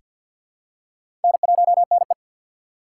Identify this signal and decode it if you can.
Morse code, heard as a keyed tone, about 48 wpm, 700 Hz. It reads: N1NE
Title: USA Call Signs - 4 Characters - 3x Character Spacing - 50wpm